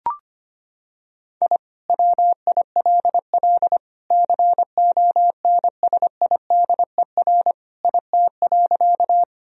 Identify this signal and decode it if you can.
Morse code, heard as a keyed tone, about 25 wpm, 700 Hz. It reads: I WILL CONSIDER IT.
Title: Sentences from Top 1000 Words - Review Part 4 - 25wpm